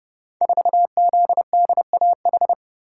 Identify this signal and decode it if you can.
Morse code, heard as a keyed tone, about 30 wpm, 700 Hz. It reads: 4ZDAH
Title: Code Groups - 5 Letter:Numbers - 30wpm